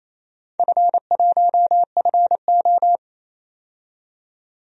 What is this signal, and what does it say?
Morse code, heard as a keyed tone, about 27 wpm, 700 Hz. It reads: F1FO
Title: DXCC-100 Call Signs - 4 Characters - 28wpm